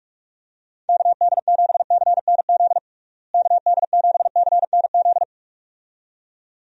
Morse code, heard as a keyed tone, about 46 wpm, 700 Hz. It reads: KD7CNZ KD7CNZ
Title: USA Call Signs - 6 Characters - Repeated 2x - 45wpm